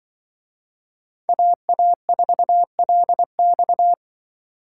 Morse code, heard as a keyed tone, about 23 wpm, 700 Hz. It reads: AA4LX
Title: Speed Racing - USA 2x2 Call Signs 20wpm